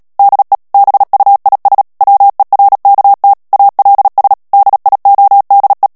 Morse code, heard as a keyed tone, about 37 wpm, 800 Hz. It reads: DE BUIS WERKT ALS DIODE
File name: morse mp3 / HOLLANDAIS texte clair 2 vitesse 35 mots minute